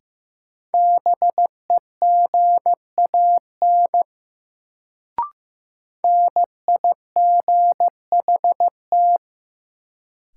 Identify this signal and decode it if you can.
Morse code, heard as a keyed tone, about 15 wpm, 700 Hz. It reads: BEGAN NIGHT
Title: Sentences from Top 300 Words - New Word Drill - 15wpm